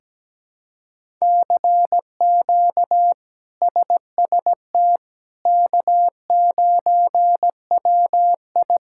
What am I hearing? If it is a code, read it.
Morse code, heard as a keyed tone, about 17 wpm, 700 Hz. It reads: CQ SST K9WI
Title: K1USN SST QSO Practice - 17wpm